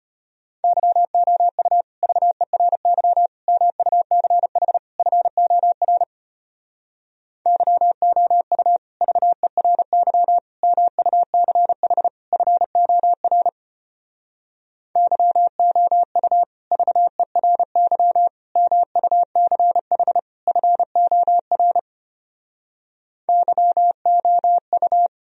Morse code, heard as a keyed tone, about 33 wpm, 700 Hz. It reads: YOU VERY MUCH FOR YOU VERY MUCH FOR YOU VERY MUCH FOR YOU